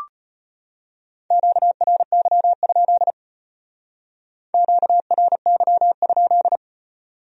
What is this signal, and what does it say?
Morse code, heard as a keyed tone, about 32 wpm, 700 Hz. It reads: QRY? QRY?